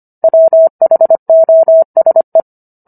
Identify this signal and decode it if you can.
Morse code, heard as a keyed tone, about 25 wpm, 650 Hz. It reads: WHOSE